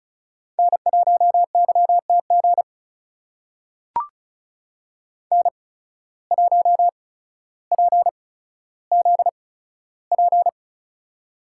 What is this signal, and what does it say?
Morse code, heard as a keyed tone, about 36 wpm, spaced as for 15 wpm, 700 Hz. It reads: N1YTG N1PZP
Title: USA Call Signs - 5 Characters - 8x Character Spacing - 35wpm